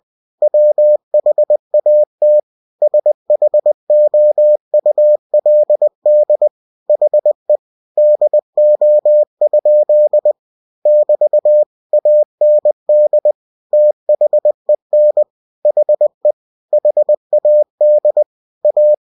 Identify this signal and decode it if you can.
Morse code, heard as a keyed tone, about 20 wpm, 600 Hz. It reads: WHAT SHOULD HE DO? <BT> AND THEN HE HAD A
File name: cw-books / winnie-the-pooh / winnie-20wpm-page-0041